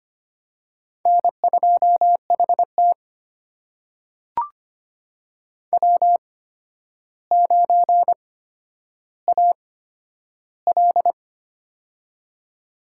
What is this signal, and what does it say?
Morse code, heard as a keyed tone, about 25 wpm, spaced as for 11 wpm, 700 Hz. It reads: N2HT W9AL
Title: USA Call Signs - 4 Characters - 8x Character Spacing - 25wpm